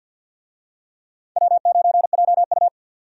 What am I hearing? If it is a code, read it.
Morse code, heard as a keyed tone, about 52 wpm, 700 Hz. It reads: W9JU
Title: USA Call Signs - 4 Characters - 50wpm